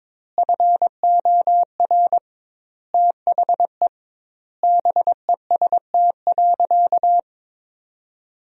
Morse code, heard as a keyed tone, about 22 wpm, 700 Hz. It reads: FOR THE BEST.
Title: 2x Word-Spacing - Sentences from Top 500 Words - 22wpm